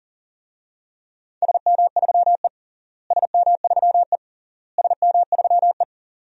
Morse code, heard as a keyed tone, about 40 wpm, 700 Hz. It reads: SM3E SM3E SM3E
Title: DXCC-100 Call Signs - 4 Characters - Repeated 3x - 40wpm